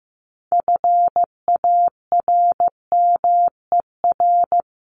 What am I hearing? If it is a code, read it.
Morse code, heard as a keyed tone, about 15 wpm, 700 Hz. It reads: FARMER
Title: Top 1751-2000 Words 15wpm